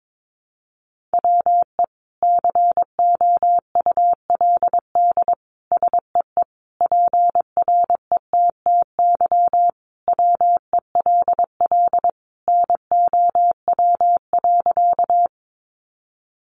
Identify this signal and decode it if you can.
Morse code, heard as a keyed tone, about 22 wpm, 700 Hz. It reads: WE COULD SEE PRETTY WELL NOW.
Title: Sentences from Top 1000 Words 22wpm